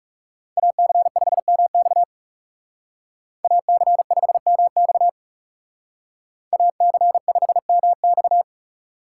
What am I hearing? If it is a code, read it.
Morse code, heard as a keyed tone, about 38 wpm, 700 Hz. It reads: AC5MX AC5MX AC5MX